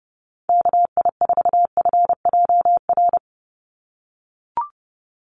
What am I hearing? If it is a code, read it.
Morse code, heard as a keyed tone, about 30 wpm, 700 Hz. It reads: KI4FJR
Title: USA Call Signs 30wpm